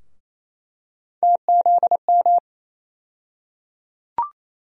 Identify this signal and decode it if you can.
Morse code, heard as a keyed tone, about 27 wpm, 700 Hz. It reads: TZM
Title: USA Call Sign Postfixes - 3 Letters 28wpm